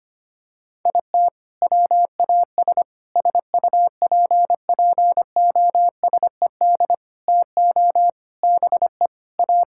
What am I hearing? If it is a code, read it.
Morse code, heard as a keyed tone, about 25 wpm, 700 Hz. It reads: IT WAS SUPPOSED TO BE A
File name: practice / 25wpm / Sentences with 4-Word Phrases - 25wpm